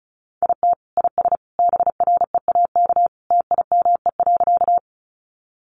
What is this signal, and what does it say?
Morse code, heard as a keyed tone, about 34 wpm, 700 Hz. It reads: IT IS BREAK TIME.
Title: Easy Phrases 35wpm